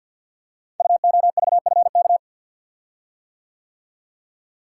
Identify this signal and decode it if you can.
Morse code, heard as a keyed tone, about 49 wpm, 700 Hz. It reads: UOVFK
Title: Code Groups - 5 Letters - 50wpm